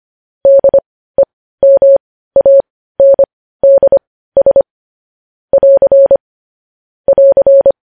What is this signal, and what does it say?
Morse code, heard as a keyed tone, about 25 wpm, spaced as for 15 wpm, 550 Hz. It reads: DEMANDS <AR> <AR>
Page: News Headlines in Morse Code at 15 WPM